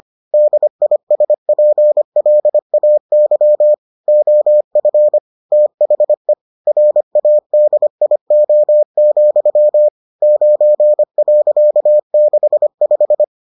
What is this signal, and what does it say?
Morse code, heard as a keyed tone, about 25 wpm, 600 Hz. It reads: DISPLAY OF THE RADIO, 9.65